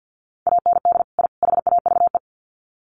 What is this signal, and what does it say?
Morse code, heard as a keyed tone, about 48 wpm, 700 Hz. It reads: AND I HAVE
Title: Sets of 3 Words 50wpm